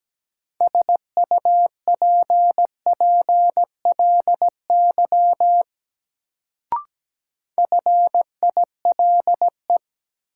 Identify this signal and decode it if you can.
Morse code, heard as a keyed tone, about 17 wpm, 700 Hz. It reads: SUPPLY FILE
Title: Sentences from Top 1000 Words - Review Part 4 - New Word Drill - 17wpm